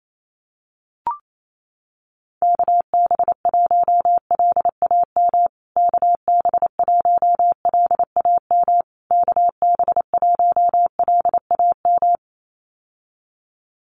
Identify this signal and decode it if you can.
Morse code, heard as a keyed tone, about 28 wpm, 700 Hz. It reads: KB1LAM KB1LAM KB1LAM
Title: USA Call Signs Repeated 3x 28wpm